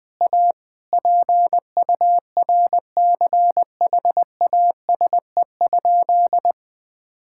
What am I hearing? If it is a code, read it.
Morse code, heard as a keyed tone, about 20 wpm, 700 Hz. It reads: A PURCHASE?
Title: Sentences from Top 1000 Words - Review Part 6 - 20wpm